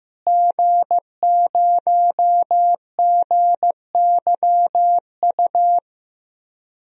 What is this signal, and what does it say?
Morse code, heard as a keyed tone, about 15 wpm, 700 Hz. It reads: G0GYU
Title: DXCC-100 Call Signs - 5 Characters - Repeated 3x - 15wpm